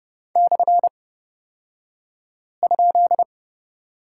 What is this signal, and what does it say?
Morse code, heard as a keyed tone, about 30 wpm, spaced as for 5 wpm, 700 Hz. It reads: /?
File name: Lesson 38d - Letter q - Two Characters 30wpm